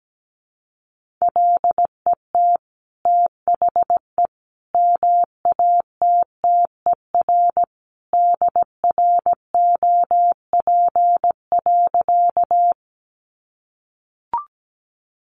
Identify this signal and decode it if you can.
Morse code, heard as a keyed tone, about 17 wpm, 700 Hz. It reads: LET THE MATTER DROP.
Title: Sentences from Top 1400 Words 17wpm